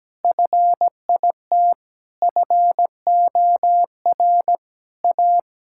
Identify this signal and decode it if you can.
Morse code, heard as a keyed tone, about 17 wpm, 700 Hz. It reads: FIT FOR A
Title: Sentences from Top 1000 Words - Review Part 5 - 17wpm